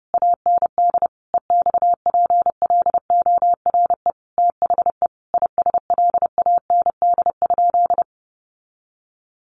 Morse code, heard as a keyed tone, about 30 wpm, 700 Hz. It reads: AND EXPLORE THE ISLAND?
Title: Sentences from Top 1100 Words 30wpm